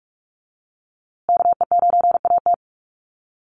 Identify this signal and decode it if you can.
Morse code, heard as a keyed tone, about 44 wpm, 700 Hz. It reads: KE9AT
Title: USA Call Signs Repeated 2x 45wpm